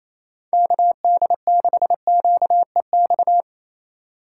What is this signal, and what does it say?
Morse code, heard as a keyed tone, about 28 wpm, 700 Hz. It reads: KD6QEX
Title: USA Call Signs - 6 Characters - 2x Character Spacing - 28wpm